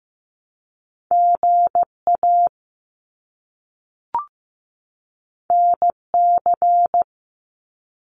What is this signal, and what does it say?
Morse code, heard as a keyed tone, about 15 wpm, 700 Hz. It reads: GA NC
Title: US States 15wpm